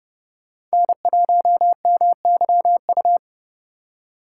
Morse code, heard as a keyed tone, about 36 wpm, 700 Hz. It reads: N1MYU
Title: Speed Racing - USA 1x3 Call Signs 30wpm